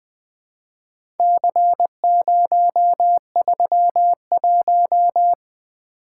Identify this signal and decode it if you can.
Morse code, heard as a keyed tone, about 20 wpm, 700 Hz. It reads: C031
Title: Four Letter-Number Combinations - 20wpm